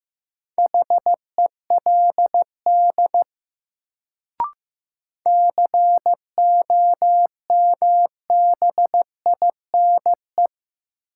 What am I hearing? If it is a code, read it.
Morse code, heard as a keyed tone, about 16 wpm, 700 Hz. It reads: HELD COMBINE
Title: Sentences from Top 1600 Words - New Word Drill Alt - 15wpm